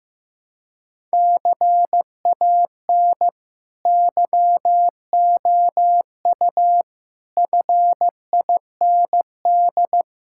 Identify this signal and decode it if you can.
Morse code, heard as a keyed tone, about 16 wpm, 700 Hz. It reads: CAN YOU FIND
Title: Sentences from Top 1600 Words 15wpm